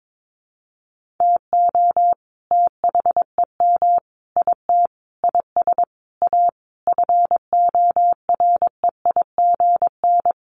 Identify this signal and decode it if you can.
Morse code, heard as a keyed tone, about 22 wpm, 700 Hz. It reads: TO THEM IT IS A FOREIGN
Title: Sentences from Top 1400 Words 22wpm